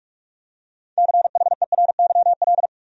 Morse code, heard as a keyed tone, about 47 wpm, 700 Hz. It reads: CHERYL
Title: USA Common Names - 45wpm